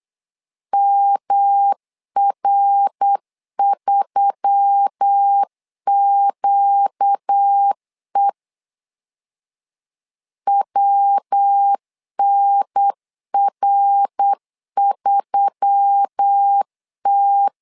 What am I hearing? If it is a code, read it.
Morse code, heard as a keyed tone, about 8 wpm, 800 Hz. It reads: MR3QE WNR3T
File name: LEZIONI / 4 Q1S3 8wpm 3037